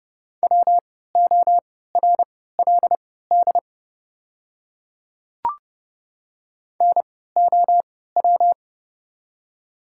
Morse code, heard as a keyed tone, about 30 wpm, spaced as for 17 wpm, 700 Hz. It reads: WORLD NOW
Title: 3x Character Spacing - Top 500 Words - 30wpm